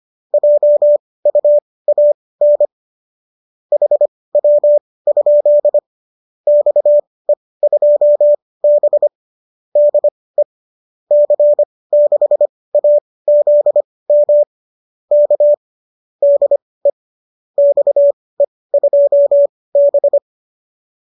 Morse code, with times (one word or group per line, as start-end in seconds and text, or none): 0.3-2.7 JUAN
3.7-5.8 HW?
6.5-9.1 XE2B
9.8-10.4 DE
11.1-14.4 C6AZM
15.1-15.5 K
16.2-16.9 DE
17.6-20.2 XE2B